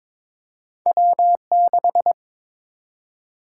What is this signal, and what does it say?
Morse code, heard as a keyed tone, about 23 wpm, 700 Hz. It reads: W6
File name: practice / 22wpm / Two Letter-Number Combinations - 22wpm